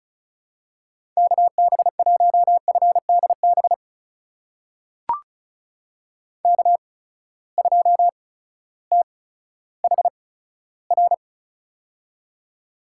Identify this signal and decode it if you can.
Morse code, heard as a keyed tone, about 36 wpm, spaced as for 16 wpm, 700 Hz. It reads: KB1FDB K2THR
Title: USA Call Signs - 8x Character Spacing - 35wpm